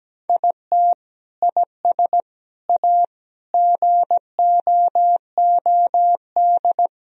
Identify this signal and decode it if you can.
Morse code, heard as a keyed tone, about 17 wpm, 700 Hz. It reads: IT IS A GOOD
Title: Sentences from Top 1000 Words - Review Part 3 - 17wpm